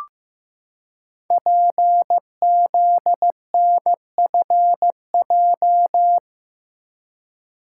Morse code, heard as a keyed tone, about 15 wpm, 700 Hz. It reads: PZNFJ
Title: Code Groups - 5 Letters - 15wpm